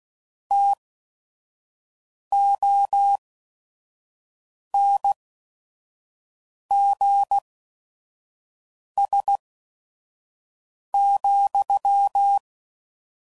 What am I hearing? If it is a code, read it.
Morse code, heard as a keyed tone, about 16 wpm, spaced as for 5 wpm, 800 Hz. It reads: TONGS,